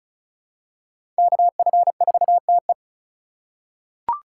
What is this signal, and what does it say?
Morse code, heard as a keyed tone, about 36 wpm, 700 Hz. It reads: KF4TE
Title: USA Call Signs - 5 Characters - 6x Character Spacing - 35wpm